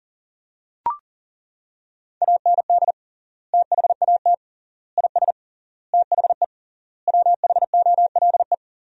Morse code, heard as a keyed tone, about 40 wpm, 700 Hz. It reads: AND THAT IS THE WHOLE
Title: 3x Word-Spacing - Sentences from Top 400 Words - 40wpm